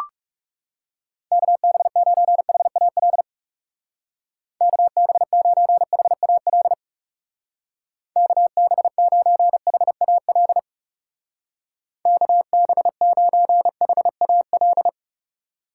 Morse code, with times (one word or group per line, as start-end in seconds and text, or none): 1.3-3.2 KB9HAL
4.6-6.7 KB9HAL
8.2-10.6 KB9HAL
12.1-14.9 KB9HAL